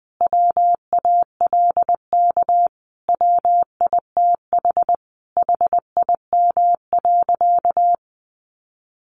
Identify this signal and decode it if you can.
Morse code, heard as a keyed tone, about 20 wpm, 700 Hz. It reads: WALK WITH HIM.
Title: Sentences from Top 1400 Words 20wpm